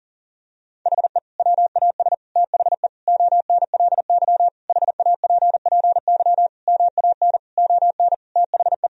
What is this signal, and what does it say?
Morse code, heard as a keyed tone, about 40 wpm, 700 Hz. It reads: HE WAS THE ONLY HAPPY MAN ON THE
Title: Sentences from Top 1000 Words - Review Part 3 - 40wpm